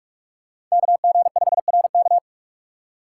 Morse code, heard as a keyed tone, about 42 wpm, 700 Hz. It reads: KG5RK
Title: USA Call Signs - 5 Characters - 2x Character Spacing - 45wpm